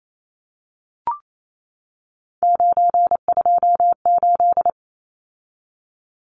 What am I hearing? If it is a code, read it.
Morse code, heard as a keyed tone, about 28 wpm, 700 Hz. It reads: 928
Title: Numbers - 3-Digit 28wpm